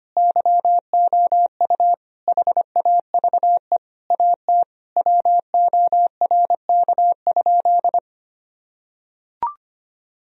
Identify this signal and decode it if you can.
Morse code, heard as a keyed tone, about 25 wpm, 700 Hz. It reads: YOU HAVE AT WORK?
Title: Sentences from Top 1000 Words - Review Part 4 - 25wpm